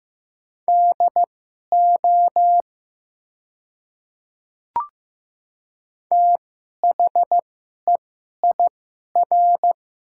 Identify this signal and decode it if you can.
Morse code, heard as a keyed tone, about 15 wpm, spaced as for 11 wpm, 700 Hz. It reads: DO THEIR
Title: 2x Character Spacing - Top 100 Words - 15wpm